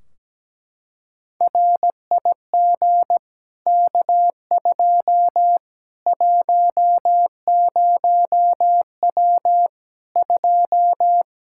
Morse code, heard as a keyed tone, about 17 wpm, 700 Hz. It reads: RIG K2 10W 2
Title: Practice QSO - Exchange 2 - Info - 17wpm